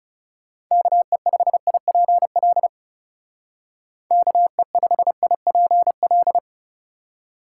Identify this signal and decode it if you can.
Morse code, heard as a keyed tone, about 36 wpm, 700 Hz. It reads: KE5IPL KE5IPL
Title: Speed Racing - USA 2x3 Call Signs 30wpm